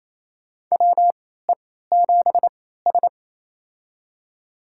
Morse code, heard as a keyed tone, about 28 wpm, spaced as for 16 wpm, 700 Hz. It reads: WE7S
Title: USA Call Signs - 4 Characters - 3x Character Spacing - 28wpm